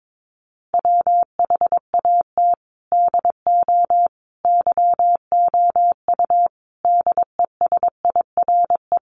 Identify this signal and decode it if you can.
Morse code, heard as a keyed tone, about 22 wpm, 700 Hz. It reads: WHAT DO YOU DESIRE
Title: Sentences from Top 1100 Words 22wpm